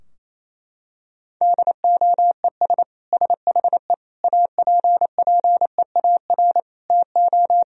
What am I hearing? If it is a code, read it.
Morse code, heard as a keyed tone, about 27 wpm, 700 Hz. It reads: DOES SHE APPEAR TO
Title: Sentences from Top 1000 Words - Review Part 2 - 28wpm